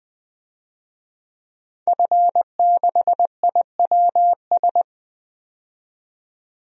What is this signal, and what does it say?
Morse code, heard as a keyed tone, about 20 wpm, 700 Hz. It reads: F6IWS
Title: DXCC-100 Call Signs - 5 Characters - 20wpm